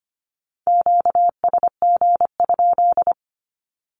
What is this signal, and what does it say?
Morse code, heard as a keyed tone, about 25 wpm, 700 Hz. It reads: QSG?
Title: Q-Codes 25wpm